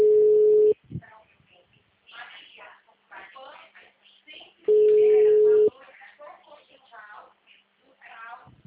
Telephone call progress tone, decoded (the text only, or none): ringback tone